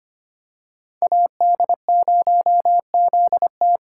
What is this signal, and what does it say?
Morse code, heard as a keyed tone, about 25 wpm, 700 Hz. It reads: AD0ZT
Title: USA Call Signs - 5 Characters - 2x Character Spacing - 25wpm